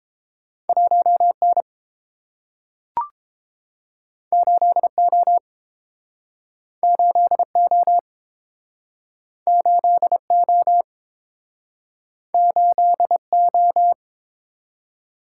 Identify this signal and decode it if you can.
Morse code, heard as a keyed tone, about 30 wpm, 700 Hz. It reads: 1N 8O 8O 8O 8O